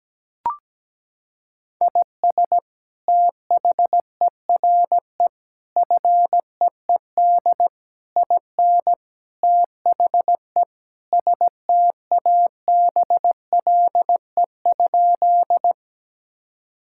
Morse code, heard as a keyed tone, about 17 wpm, 700 Hz. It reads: IS THERE FEED IN THE STABLE?
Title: Sentences from Top 1600 Words 17wpm